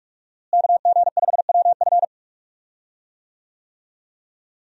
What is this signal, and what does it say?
Morse code, heard as a keyed tone, about 42 wpm, 700 Hz. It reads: KG5WF
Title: USA Call Signs - 5 Characters - Repeated 3x - 45wpm